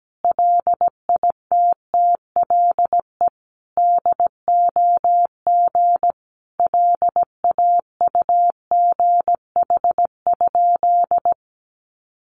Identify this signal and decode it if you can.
Morse code, heard as a keyed tone, about 17 wpm, 700 Hz. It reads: LITTLE DOG LAUGH?
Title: Sentences from Top 500 Words 17wpm